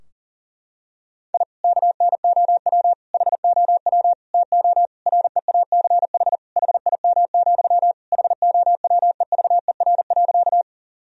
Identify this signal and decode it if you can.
Morse code, heard as a keyed tone, about 40 wpm, 700 Hz. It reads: I KNOW HOW TO REACH HIM, HOWEVER.
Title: Sentences from Top 1000 Words - Review Part 2 - 40wpm